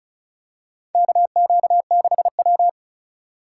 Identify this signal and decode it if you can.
Morse code, heard as a keyed tone, about 36 wpm, 700 Hz. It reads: KQ6W
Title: USA Call Signs - 3x Character Spacing - 35wpm